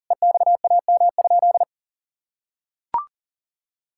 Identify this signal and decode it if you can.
Morse code, heard as a keyed tone, about 40 wpm, 700 Hz. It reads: EXAM?
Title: Sentences from Top 1000 Words - Review Part 5 - 40wpm